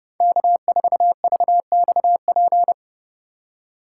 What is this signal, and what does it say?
Morse code, heard as a keyed tone, about 30 wpm, 700 Hz. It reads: K4VXP
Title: USA Call Signs - 5 Characters - 4x Character Spacing - 30wpm